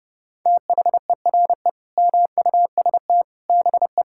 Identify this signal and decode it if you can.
Morse code, heard as a keyed tone, about 30 wpm, 700 Hz. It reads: THERE MUST BE
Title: Sentences from Top 1000 Words - Review Part 6 - 30wpm